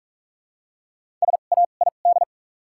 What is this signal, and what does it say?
Morse code, heard as a keyed tone, about 43 wpm, spaced as for 32 wpm, 700 Hz. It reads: SAID
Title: 2x Character Spacing - Top 200 Words - 45wpm